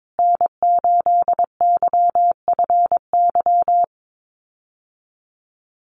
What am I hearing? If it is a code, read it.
Morse code, heard as a keyed tone, about 22 wpm, 700 Hz. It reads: N8YFY
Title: USA Call Signs Repeated 2x 22wpm